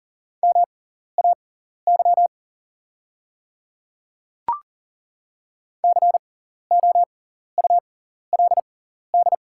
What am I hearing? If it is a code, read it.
Morse code, heard as a keyed tone, about 40 wpm, spaced as for 14 wpm, 690 Hz. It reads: MAY COULD
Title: 6x Character Spacing - Top 200 Words - 40wpm